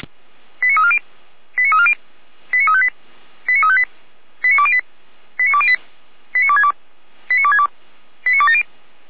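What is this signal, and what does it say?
Five-tone selective call (ZVEI), heard as quick runs of five tones, tones of about 70 ms. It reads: 89340 89340 89358 89358 89208 89208 89262 89262 89270
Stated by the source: ZVEI five-tone selective call 89330 received on SDR.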